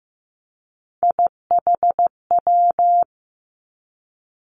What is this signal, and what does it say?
Morse code, heard as a keyed tone, about 15 wpm, 700 Hz. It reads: IHW
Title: Three Challenging Letter-Number Combinations 15wpm